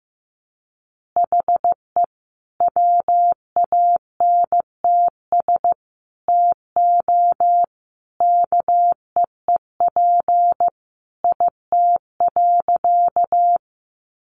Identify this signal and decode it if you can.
Morse code, heard as a keyed tone, about 15 wpm, 700 Hz. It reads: HE WANTS TO KEEP IT.
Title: Additional Easy Phrases 15wpm